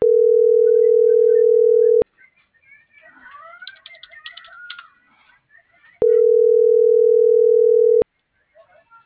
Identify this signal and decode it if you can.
Ringback tone (North America), 440 + 480 Hz.